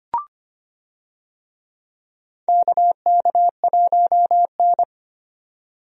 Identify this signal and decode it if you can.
Morse code, heard as a keyed tone, about 25 wpm, 700 Hz. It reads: KK1N
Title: USA Call Signs - 4 Characters - 25wpm